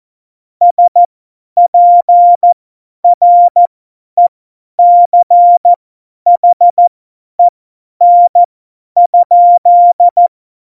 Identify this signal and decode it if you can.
Morse code, heard as a keyed tone, about 14 wpm, spaced as for 10 wpm, 700 Hz. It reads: SPRECHEN?
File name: Morsekurs / Witze / Witz 06. November 2018, 50 F70